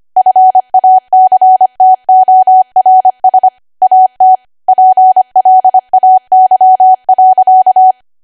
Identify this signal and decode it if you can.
Morse code, heard as a keyed tone, about 25 wpm, 750 Hz. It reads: FACTORS AT PLAY.
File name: Archive / 25 WPM / 25WPM